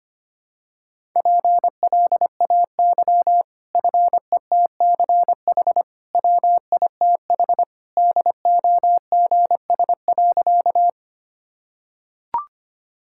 Morse code, heard as a keyed tone, about 25 wpm, 700 Hz. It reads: PLAY FETCH WITH DOGS.
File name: Easy Sentences with Hints - 25wpm